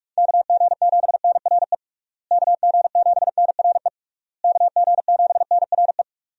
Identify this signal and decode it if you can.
Morse code, heard as a keyed tone, about 44 wpm, 700 Hz. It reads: KG7NRE KG7NRE KG7NRE
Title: USA Call Signs - 6 Characters - Repeated 3x - 45wpm